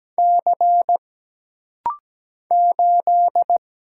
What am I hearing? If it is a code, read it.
Morse code, heard as a keyed tone, about 17 wpm, 700 Hz. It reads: C 8
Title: Single Letter-Number - Rapid-Fire 17wpm